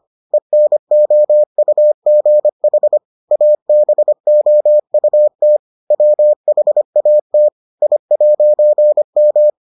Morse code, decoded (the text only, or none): ENOUGH ABOUT WHAT I'M